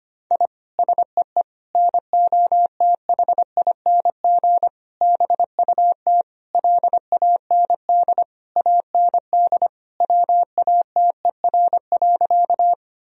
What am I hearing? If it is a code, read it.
Morse code, heard as a keyed tone, about 25 wpm, 700 Hz. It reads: I SEE NOTHING BUT LAND AND WATER.
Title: Sentences from Top 1000 Words - Review Part 1 - 25wpm